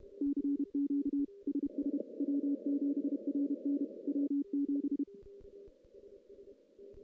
Morse code, heard as a keyed tone, about 32 wpm, 280 Hz. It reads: CQ SS W7RN W7
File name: contests / CW-SS-k5tr- -1500